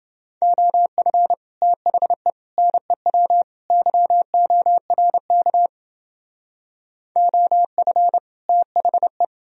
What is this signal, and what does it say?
Morse code, heard as a keyed tone, about 28 wpm, 700 Hz. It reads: OF THE NEW YORK OF THE